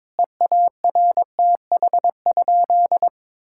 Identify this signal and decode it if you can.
Morse code, heard as a keyed tone, about 22 wpm, 700 Hz. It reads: EARTH?
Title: Sentences from Top 1600 Words 22wpm